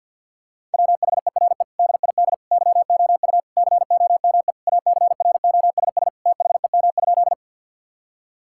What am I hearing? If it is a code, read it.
Morse code, heard as a keyed tone, about 50 wpm, 700 Hz. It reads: WHERE DID YOU COME ACROSS THEM?
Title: Sentences from Top 1000 Words - Review Part 5 - 50wpm